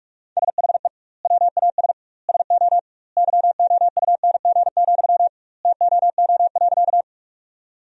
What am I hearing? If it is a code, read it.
Morse code, heard as a keyed tone, about 46 wpm, 700 Hz. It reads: SHE WAS SO YOUNG, TOO.
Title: 2x Word-Spacing - Sentences from Top 400 Words - 45wpm